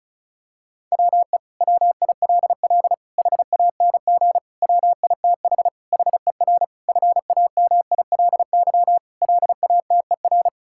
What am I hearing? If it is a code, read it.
Morse code, heard as a keyed tone, about 34 wpm, 700 Hz. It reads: WE WILL HANG WITH HER FAMILY LATER
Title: Sentences from Top 1600 Words 35wpm